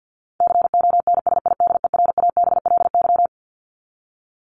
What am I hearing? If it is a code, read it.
Morse code, heard as a keyed tone, about 50 wpm, 700 Hz. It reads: CONSIDERABLY
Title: LY Suffix - Top 100 Words with Base Words 50wpm